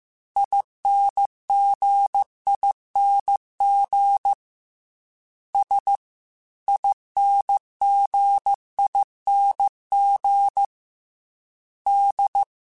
Morse code, heard as a keyed tone, about 15 wpm, 800 Hz. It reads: INGING S INGING D